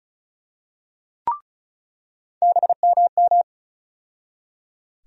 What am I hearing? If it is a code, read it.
Morse code, heard as a keyed tone, about 34 wpm, 700 Hz. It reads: BMM
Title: USA Call Sign Postfixes - 3 Letters 35wpm